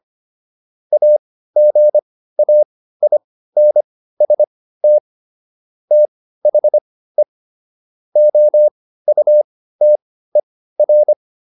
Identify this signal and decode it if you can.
Morse code, heard as a keyed tone, about 25 wpm, spaced as for 15 wpm, 600 Hz. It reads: AGAINST THE OUTER
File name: SS203 15